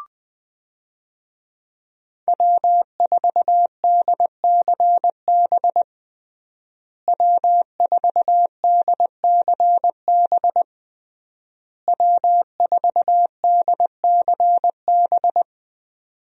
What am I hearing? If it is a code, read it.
Morse code, heard as a keyed tone, about 20 wpm, 700 Hz. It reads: W4DCB W4DCB W4DCB
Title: USA Call Signs - 5 Characters - Repeated 3x - 20wpm